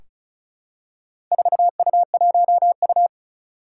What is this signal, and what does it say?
Morse code, heard as a keyed tone, about 36 wpm, 700 Hz. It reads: 4U1U
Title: Four Challenging Letter-Number Combinations 35wpm